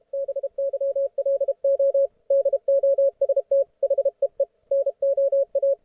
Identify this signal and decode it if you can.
Morse code, heard as a keyed tone, about 33 wpm, 560 Hz, with radio noise. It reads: BYLO DOST HEE NO A